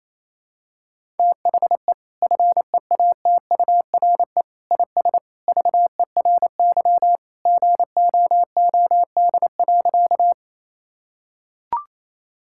Morse code, heard as a keyed tone, about 28 wpm, 700 Hz. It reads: THE FEATURE IS VERY GOOD.
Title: Sentences from Top 1000 Words - Review Part 5 - 28wpm